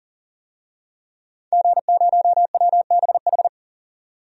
Morse code, heard as a keyed tone, about 40 wpm, 700 Hz. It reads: G0WBH
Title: DXCC-100 Call Signs - 5 Characters - 40wpm